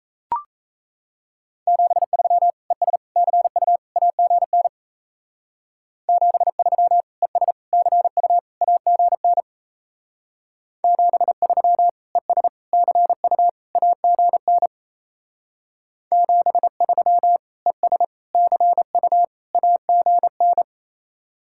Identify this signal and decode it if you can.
Morse code, heard as a keyed tone, about 34 wpm, 700 Hz. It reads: 73 ES CU AGN 73 ES CU AGN 73 ES CU AGN 73 ES CU AGN
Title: Speed-Racing Common QSO Phrases Encore - 28wpm